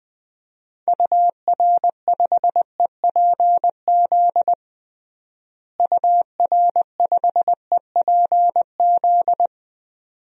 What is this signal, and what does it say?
Morse code, heard as a keyed tone, about 20 wpm, 700 Hz. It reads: UR5EPZ UR5EPZ
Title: DXCC-100 Call Signs - 6 Characters - Repeated 2x - 20wpm